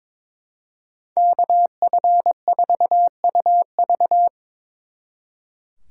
Morse code, heard as a keyed tone, about 26 wpm, 700 Hz. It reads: KF4UV
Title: Speed Racing - USA 2x2 Call Signs 22wpm